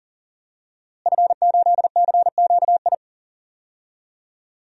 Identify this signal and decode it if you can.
Morse code, heard as a keyed tone, about 40 wpm, 700 Hz. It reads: F8CQI
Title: DXCC-100 Call Signs - 5 Characters - Repeated 3x - 40wpm